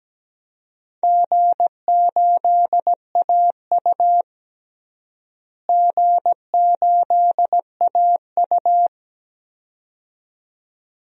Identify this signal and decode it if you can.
Morse code, heard as a keyed tone, about 17 wpm, 700 Hz. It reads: G8AU G8AU
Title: DXCC-100 Call Signs - 4 Characters - Repeated 2x - 17wpm